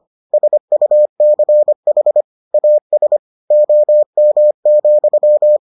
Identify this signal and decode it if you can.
Morse code, heard as a keyed tone, about 25 wpm, 600 Hz. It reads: SUCH AS OM,